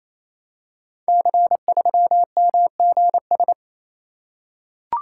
Morse code, heard as a keyed tone, about 28 wpm, 700 Hz. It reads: C3MGS
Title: Code Groups - 5 Letter:Numbers - 28wpm